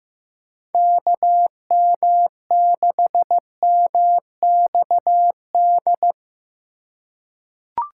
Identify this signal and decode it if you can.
Morse code, heard as a keyed tone, about 15 wpm, 700 Hz. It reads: KM6MXD
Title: USA Call Signs - 6 Characters - 8x Character Spacing - 15wpm